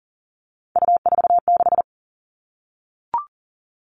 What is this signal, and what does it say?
Morse code, heard as a keyed tone, about 40 wpm, 700 Hz. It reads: U46
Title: All Cut Numbers - 3-Digit 40wpm